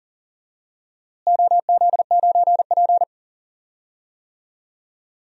Morse code, heard as a keyed tone, about 40 wpm, 700 Hz. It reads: OZ9P
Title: DXCC-100 Call Signs - 4 Characters - Repeated 3x - 40wpm